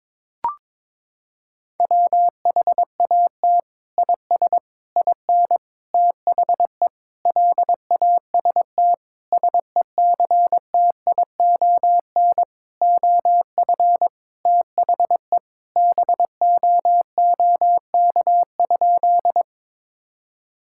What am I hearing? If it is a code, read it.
Morse code, heard as a keyed tone, about 22 wpm, 700 Hz. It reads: WHAT IS IN THE LAST SECTION OF THE BOOK?
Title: Sentences from Top 1000 Words - Review Part 5 - 22wpm